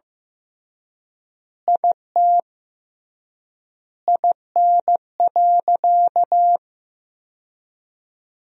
Morse code, heard as a keyed tone, about 15 wpm, 700 Hz. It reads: IT IN.
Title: 3x Word-Spacing - Sentences from Top 200 Words - 15wpm